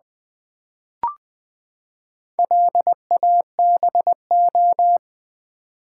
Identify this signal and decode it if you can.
Morse code, heard as a keyed tone, about 20 wpm, 700 Hz. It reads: LABO
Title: Build A Word - 4 Letters At A Time - 10-Letter Words - 20wpm